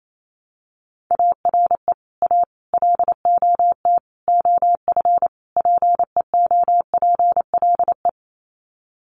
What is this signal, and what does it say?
Morse code, heard as a keyed tone, about 28 wpm, 700 Hz. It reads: ARE A LOT OF PEOPLE
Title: Sets of 5 Words 28wpm